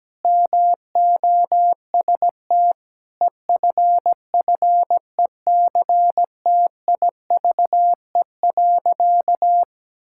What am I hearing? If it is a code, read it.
Morse code, heard as a keyed tone, about 17 wpm, 700 Hz. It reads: MOST EFFECTIVE.
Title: Sentences from Top 1000 Words - Review Part 6 - 17wpm